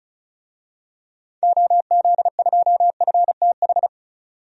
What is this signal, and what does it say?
Morse code, heard as a keyed tone, about 34 wpm, 700 Hz. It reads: OZ2FTH
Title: DXCC-100 Call Signs - 6 Characters - 35wpm